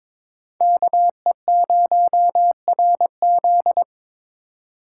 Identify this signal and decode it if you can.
Morse code, heard as a keyed tone, about 26 wpm, 700 Hz. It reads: KE0RZ
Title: Speed Racing - USA Call Signs 22wpm